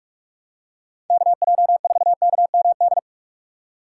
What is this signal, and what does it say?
Morse code, heard as a keyed tone, about 46 wpm, 700 Hz. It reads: KJ4KMD
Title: USA Call Signs - 6 Characters - 2x Character Spacing - 45wpm